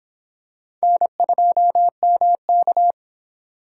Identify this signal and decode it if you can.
Morse code, heard as a keyed tone, about 26 wpm, 700 Hz. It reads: N2MK